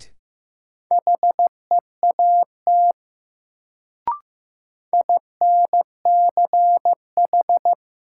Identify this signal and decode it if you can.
Morse code, heard as a keyed tone, about 15 wpm, 700 Hz. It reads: HEAT INCH
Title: Sentences from Top 1000 Words - Review Part 3 - New Word Drill - 15wpm